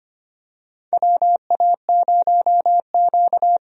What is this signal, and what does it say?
Morse code, heard as a keyed tone, about 25 wpm, 700 Hz. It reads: WA0Q
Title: USA Call Signs - 4 Characters - 25wpm